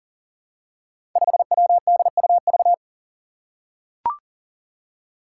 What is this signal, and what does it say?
Morse code, heard as a keyed tone, about 38 wpm, 700 Hz. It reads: 5WDUV
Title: Five Challenging Letter-Number Combinations 40wpm